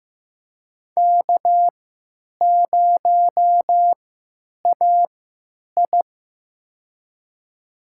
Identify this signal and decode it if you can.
Morse code, heard as a keyed tone, about 15 wpm, spaced as for 9 wpm, 700 Hz. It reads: K0AI
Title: USA Call Signs - 3x Character Spacing - 15wpm